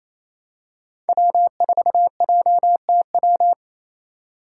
Morse code, heard as a keyed tone, about 28 wpm, 700 Hz. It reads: W4JTW
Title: USA Call Signs - 5 Characters - 4x Character Spacing - 28wpm